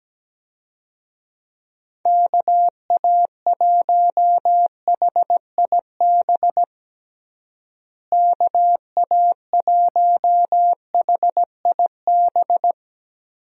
Morse code, heard as a keyed tone, about 17 wpm, 700 Hz. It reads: KA1HIB KA1HIB
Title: USA Call Signs - 6 Characters - Repeated 2x - 17wpm